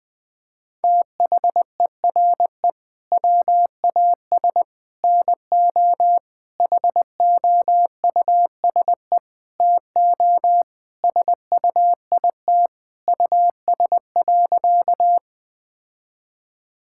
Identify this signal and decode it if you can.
Morse code, heard as a keyed tone, about 21 wpm, 700 Hz. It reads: THERE WAS NO HOUSE TO SUIT US.
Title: Sentences from Top 1600 Words 20wpm